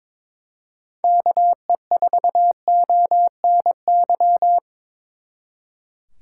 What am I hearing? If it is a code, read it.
Morse code, heard as a keyed tone, about 24 wpm, 700 Hz. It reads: KE4ONY
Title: Speed Racing - USA 2x3 Call Signs 22wpm